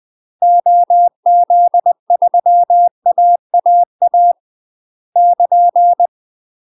Morse code, heard as a keyed tone, about 20 wpm, 700 Hz. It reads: OZ3AAA <KN>
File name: UK MASTER QSOs TRAINING / 20WPM / UK MASTER QSOs 0066